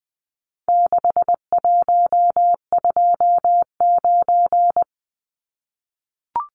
Morse code, heard as a keyed tone, about 20 wpm, 700 Hz. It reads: 6129
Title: Numbers - 4-Digit 20wpm